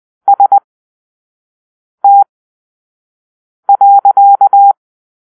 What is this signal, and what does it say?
Morse code, heard as a keyed tone, about 20 wpm, spaced as for 5 wpm, 800 Hz. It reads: ST.
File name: arrl / Grp-6 Drill-3